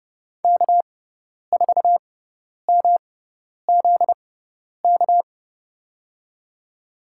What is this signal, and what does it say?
Morse code, heard as a keyed tone, about 30 wpm, spaced as for 10 wpm, 700 Hz. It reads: K4MZK
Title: USA Call Signs - 5 Characters - 6x Character Spacing - 30wpm